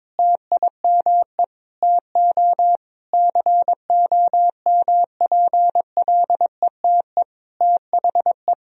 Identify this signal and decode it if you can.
Morse code, heard as a keyed tone, about 22 wpm, 700 Hz. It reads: TIME TO COMPLETE THE
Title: Sentences from Top 1000 Words - Review Part 1 - 22wpm